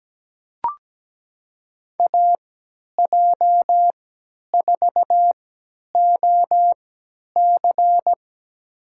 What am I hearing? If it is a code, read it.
Morse code, heard as a keyed tone, about 17 wpm, spaced as for 10 wpm, 700 Hz. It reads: AJ4OC
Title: USA Call Signs - 3x Character Spacing - 17wpm